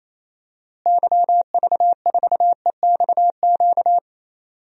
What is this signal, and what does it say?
Morse code, heard as a keyed tone, about 28 wpm, 700 Hz. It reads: YV4EXQ
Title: DXCC-100 Call Signs - 6 Characters - Repeated 2x - 28wpm